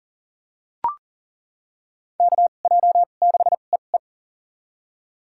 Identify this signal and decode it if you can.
Morse code, heard as a keyed tone, about 40 wpm, spaced as for 29 wpm, 700 Hz. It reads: KJ6EE
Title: USA Call Signs - 5 Characters - 2x Character Spacing - 40wpm